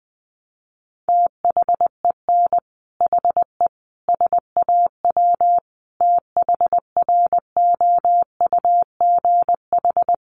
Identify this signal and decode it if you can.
Morse code, heard as a keyed tone, about 20 wpm, 700 Hz. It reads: THEN HE SAW THROUGH
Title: Sentences from Top 400 Words 20wpm